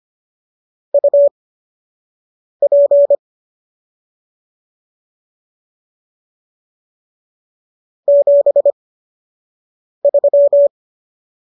Morse code, heard as a keyed tone, about 25 wpm, spaced as for 6 wpm, 570 Hz. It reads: UP 73